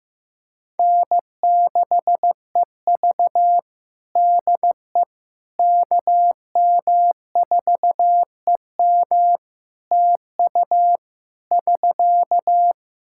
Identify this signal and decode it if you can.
Morse code, heard as a keyed tone, about 15 wpm, 700 Hz. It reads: N6EV DE KM4EM TU <SK>
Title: Practice QSO - Exchange 4 - Ending - 15wpm